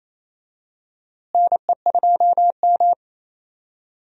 Morse code, heard as a keyed tone, about 28 wpm, 700 Hz. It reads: NE2M
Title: USA Call Signs - 4 Characters - Repeated 2x - 28wpm